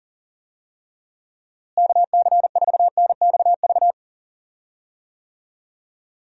Morse code, heard as a keyed tone, about 40 wpm, 700 Hz. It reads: KC4NXV
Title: USA Call Signs - 6 Characters - Repeated 3x - 40wpm